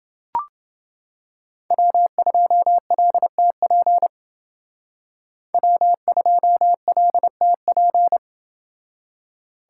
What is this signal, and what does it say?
Morse code, heard as a keyed tone, about 28 wpm, 700 Hz. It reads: W2LTP W2LTP